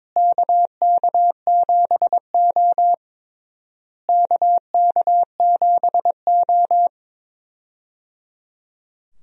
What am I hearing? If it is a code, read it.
Morse code, heard as a keyed tone, about 22 wpm, 700 Hz. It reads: KK7O KK7O
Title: USA Call Signs - 4 Characters - Repeated 3x - 22wpm